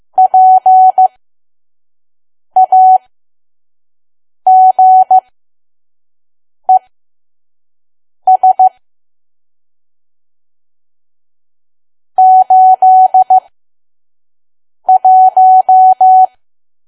Morse code, heard as a keyed tone, about 15 wpm, spaced as for 5 wpm, 750 Hz. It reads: PAGES 81
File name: Archive / 5 WPM files / PM